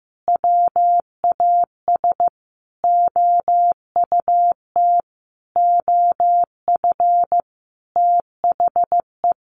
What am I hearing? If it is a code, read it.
Morse code, heard as a keyed tone, about 14 wpm, 700 Hz. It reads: WAS OUT OF THE
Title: Sentences from Top 600 Words 15wpm